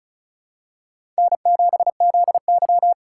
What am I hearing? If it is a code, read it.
Morse code, heard as a keyed tone, about 36 wpm, 700 Hz. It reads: N7ZY
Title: USA Call Signs - 4x Character Spacing - 35wpm